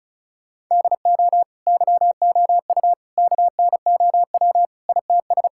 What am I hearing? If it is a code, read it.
Morse code, heard as a keyed tone, about 36 wpm, 700 Hz. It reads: DO YOU KNOW ITS
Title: Sentences from Top 1000 Words - Review Part 5 - 35wpm